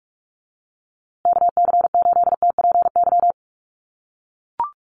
Morse code, heard as a keyed tone, about 44 wpm, 700 Hz. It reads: KC8TPY
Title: USA Call Signs Repeated 3x 45wpm